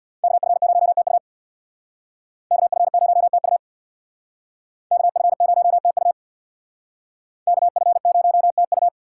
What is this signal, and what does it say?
Morse code, heard as a keyed tone, about 57 wpm, 700 Hz. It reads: KF0TU KF0TU KF0TU KF0TU
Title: Speed Racing - USA 2x2 Call Signs 50wpm